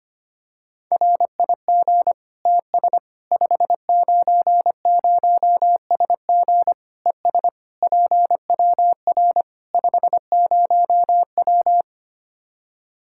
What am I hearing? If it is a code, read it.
Morse code, heard as a keyed tone, about 25 wpm, 700 Hz. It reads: RIG TS 590SG ES PWR 50W
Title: Practice QSO - Exchange 2 - Info - 25wpm